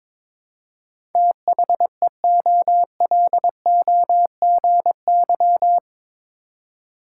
Morse code, heard as a keyed tone, about 22 wpm, 700 Hz. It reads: THEOLOGY